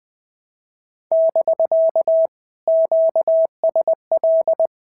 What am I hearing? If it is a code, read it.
Morse code, heard as a keyed tone, about 20 wpm, 650 Hz. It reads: <BK> QSL
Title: POTA QSO Practice - 20wpm